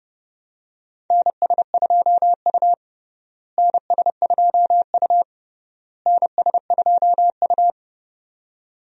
Morse code, heard as a keyed tone, about 30 wpm, 700 Hz. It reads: NS2U NS2U NS2U
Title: USA Call Signs - 4 Characters - Repeated 3x - 30wpm